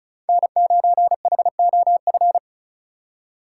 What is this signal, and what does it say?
Morse code, heard as a keyed tone, about 36 wpm, 700 Hz. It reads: N9HOF
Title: USA Call Signs - 5 Characters - 4x Character Spacing - 35wpm